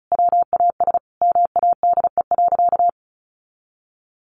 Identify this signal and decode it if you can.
Morse code, heard as a keyed tone, about 34 wpm, 700 Hz. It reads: WAS MADE.
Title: Sentences from Top 1500 Words 35wpm